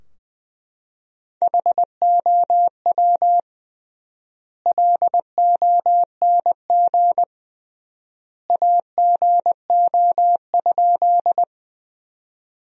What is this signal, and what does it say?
Morse code, heard as a keyed tone, about 20 wpm, 700 Hz. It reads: HOW LONG AGO?
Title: 3x Word-Spacing - Sentences from Top 500 Words - 20wpm